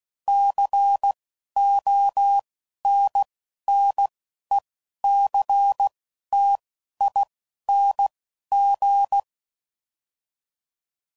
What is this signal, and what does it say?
Morse code, heard as a keyed tone, about 16 wpm, 800 Hz. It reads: C O N N E C T I N G
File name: Suffixes / Special INGs / Single INGs